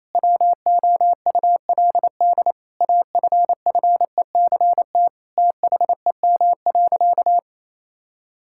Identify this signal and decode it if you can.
Morse code, heard as a keyed tone, about 28 wpm, 700 Hz. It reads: WOULD AFFECT THEM.
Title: Sentences from Top 1000 Words - Review Part 6 - 28wpm